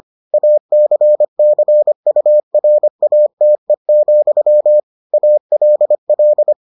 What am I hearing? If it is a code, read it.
Morse code, heard as a keyed tone, about 25 wpm, 600 Hz. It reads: ACCURATE, ALL